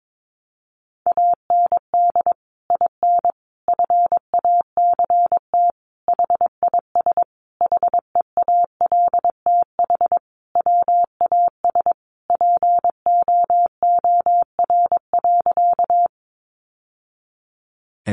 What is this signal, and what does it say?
Morse code, heard as a keyed tone, about 22 wpm, 700 Hz. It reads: AND IN FACT HIS HEALTH WAS POOR.
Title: Sentences from Top 1000 Words 22wpm